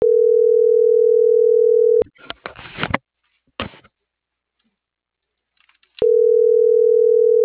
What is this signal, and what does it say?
Ringback tone (North America), 440 + 480 Hz.